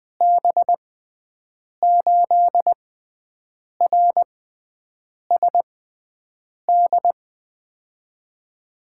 Morse code, heard as a keyed tone, about 20 wpm, spaced as for 7 wpm, 700 Hz. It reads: B8RSD
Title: USA Call Signs - 6 Characters - 6x Character Spacing - 20wpm